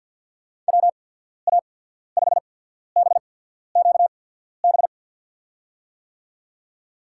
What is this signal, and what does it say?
Morse code, heard as a keyed tone, about 53 wpm, spaced as for 14 wpm, 700 Hz. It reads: WA5BQB
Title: USA Call Signs - 6 Characters - 8x Character Spacing - 50wpm